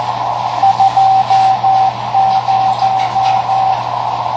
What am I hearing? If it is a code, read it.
Morse code, heard as a keyed tone, about 14 wpm, 770 Hz, with radio noise. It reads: 20
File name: tracks / KA9SZX 3W Macomb IL